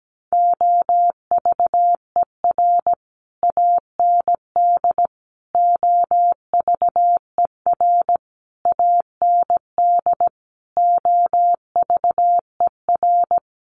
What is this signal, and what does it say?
Morse code, heard as a keyed tone, about 17 wpm, 700 Hz. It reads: OVER AND OVER AND OVER
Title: Sets of 5 Words Encore 17wpm